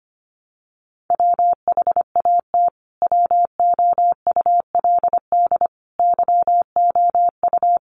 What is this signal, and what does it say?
Morse code, heard as a keyed tone, about 25 wpm, 700 Hz. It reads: WHAT WOULD YOU
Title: Sentences from Top 300 Words 25wpm